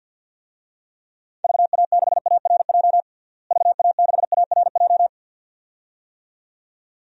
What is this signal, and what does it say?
Morse code, heard as a keyed tone, about 53 wpm, 700 Hz. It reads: VA6ARJ VA6ARJ
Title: DXCC-100 Call Signs - 6 Characters - Repeated 2x - 50wpm